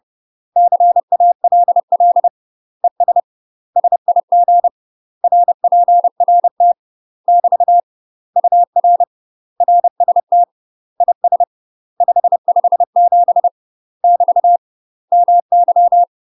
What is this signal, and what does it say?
Morse code, heard as a keyed tone, about 30 wpm, 700 Hz. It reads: CALL ES SIG RPRT <BT> UR RST IS 557 <BT> MY
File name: UK MASTER QSOs TRAINING / 30WPM / UK MASTER QSOs 0002